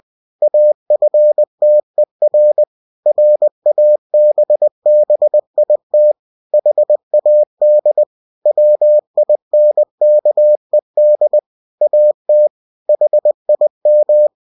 Morse code, heard as a keyed tone, about 20 wpm, 600 Hz. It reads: AFTER RABBIT HAD WINKED AT HIM
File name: cw-books / winnie-the-pooh / winnie-20wpm-page-0066